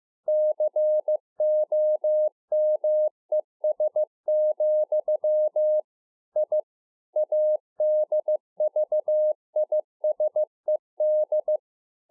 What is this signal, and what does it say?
Morse code, heard as a keyed tone, about 15 wpm, 610 Hz. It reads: COMES, I ADVISED